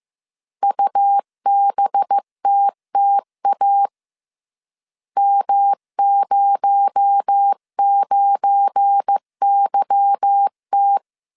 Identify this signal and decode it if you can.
Morse code, heard as a keyed tone, about 15 wpm, 800 Hz. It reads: UBTTA M09YT
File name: CW / DET / 1 9 1500C S A 14wpm 3222